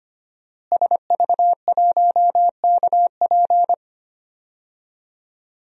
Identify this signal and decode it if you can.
Morse code, heard as a keyed tone, about 25 wpm, 700 Hz. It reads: SV1KP
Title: DXCC-100 Call Signs - 5 Characters - Repeated 2x - 25wpm